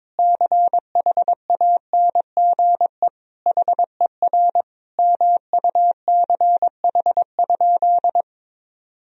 Morse code, heard as a keyed tone, about 22 wpm, 700 Hz. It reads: CHANGE HER MUCH?
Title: Sentences from Top 1000 Words - Review Part 1 - 22wpm